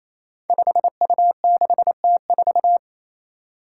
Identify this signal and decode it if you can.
Morse code, heard as a keyed tone, about 28 wpm, 700 Hz. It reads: 5U6T4
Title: Five Challenging Letter-Number Combinations 28wpm